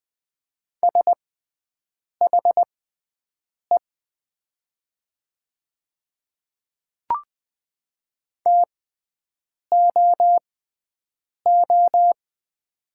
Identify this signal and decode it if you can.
Morse code, heard as a keyed tone, about 20 wpm, spaced as for 7 wpm, 700 Hz. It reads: SHE TOO
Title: 6x Character Spacing - Top 300 Words - 20wpm